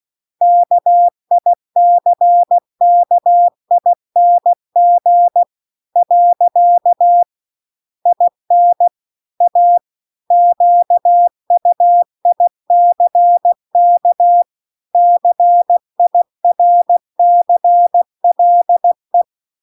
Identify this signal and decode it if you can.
Morse code, heard as a keyed tone, about 16 wpm, 700 Hz. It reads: KICKING . IN A QUICK CIRCLE
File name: CW-The-Big-Twoheaded-River-2-16WPM